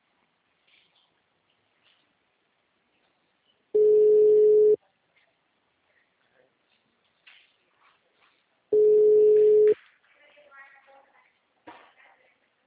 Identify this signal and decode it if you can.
Ringback tone (Europe, 425 Hz), with speech.